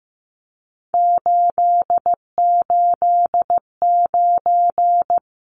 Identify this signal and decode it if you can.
Morse code, heard as a keyed tone, about 15 wpm, 700 Hz. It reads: 889
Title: Numbers - 3-Digit 15wpm